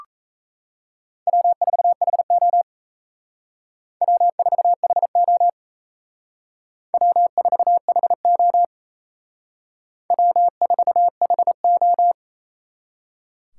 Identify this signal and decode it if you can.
Morse code, heard as a keyed tone, about 35 wpm, 700 Hz. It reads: W4HO W4HO W4HO W4HO
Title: Speed Racing - USA 1x2 Call Signs 28wpm